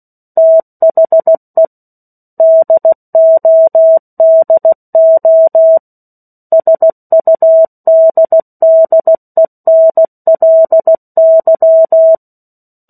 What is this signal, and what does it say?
Morse code, heard as a keyed tone, about 16 wpm, 650 Hz. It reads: THE DODO SUDDENLY
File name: 03-Chapter-03 16 WPM